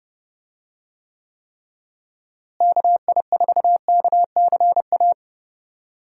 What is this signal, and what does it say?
Morse code, heard as a keyed tone, about 29 wpm, 700 Hz. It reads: KI4KCA
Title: USA Call Signs - 6 Characters - Repeated 3x - 30wpm